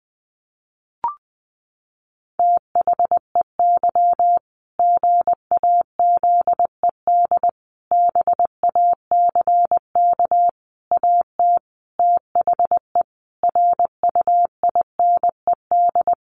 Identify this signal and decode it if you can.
Morse code, heard as a keyed tone, about 20 wpm, 700 Hz. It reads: THEY GAZED BACK AT THE RUINED
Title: A Sound Of Thunder 20wpm